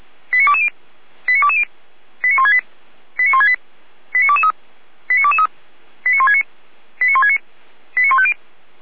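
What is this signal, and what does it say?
Five-tone selective call (ZVEI), heard as quick runs of five tones, tones of about 70 ms. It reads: 89220 89220 89168 89168 89223 89223 89169 89169 89150
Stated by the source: Five-tone ZVEI selective call received by SDR, from feuerwehr-orlamuende.org (Orlamünde fire brigade).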